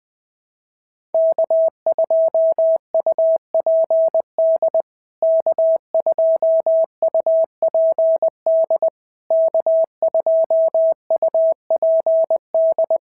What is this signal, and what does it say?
Morse code, heard as a keyed tone, about 20 wpm, 650 Hz. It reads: K2UPD K2UPD K2UPD
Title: POTA QSO Practice - 20wpm